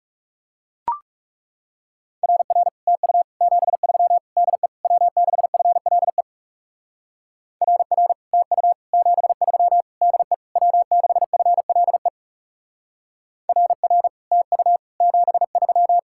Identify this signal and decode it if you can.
Morse code, heard as a keyed tone, about 36 wpm, 700 Hz. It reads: RR TU 73 DE W6FLE RR TU 73 DE W6FLE RR TU 73
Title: Speed-Racing Common QSO Phrases - 30wpm